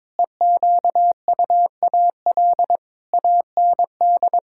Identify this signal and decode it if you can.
Morse code, heard as a keyed tone, about 21 wpm, 700 Hz. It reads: EQUAL AND
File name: practice / 22wpm / Sentences with 8-Word Phrases - 22wpm